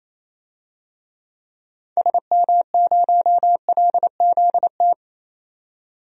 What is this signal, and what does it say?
Morse code, heard as a keyed tone, about 28 wpm, 700 Hz. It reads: SM0LZT
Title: DXCC-100 Call Signs - 6 Characters - 28wpm